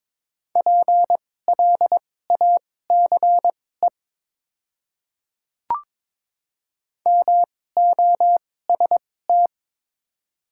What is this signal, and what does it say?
Morse code, heard as a keyed tone, about 22 wpm, spaced as for 16 wpm, 700 Hz. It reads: PLACE MOST
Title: 2x Character Spacing - Top 500 Words - 22wpm